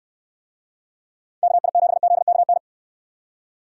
Morse code, heard as a keyed tone, about 65 wpm, 700 Hz. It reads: KI7ZKN